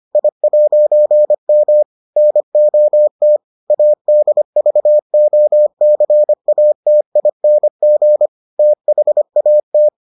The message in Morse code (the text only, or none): I'M NOT ADVOCATING THAT